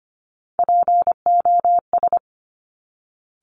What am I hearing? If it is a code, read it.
Morse code, heard as a keyed tone, about 25 wpm, 700 Hz. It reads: POS
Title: Top Three Letter Combinations in Words 25wpm